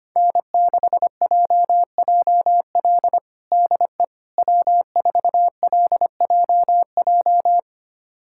Morse code, heard as a keyed tone, about 25 wpm, 700 Hz. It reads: N6JJL DE W4LJJ
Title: USA Callsign DE Callsign - 5 Characters - 25wpm